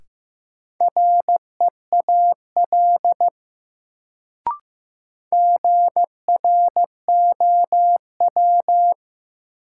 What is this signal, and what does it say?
Morse code, heard as a keyed tone, about 15 wpm, 700 Hz. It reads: REAL GROW
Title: Sentences from Top 300 Words - New Word Drill - 15wpm